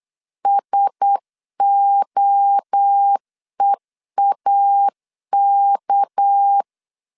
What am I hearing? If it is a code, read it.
Morse code, heard as a keyed tone, about 8 wpm, 800 Hz. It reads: SOEAK
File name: CW / DET / 1 6 1500C S B 8wpm 5142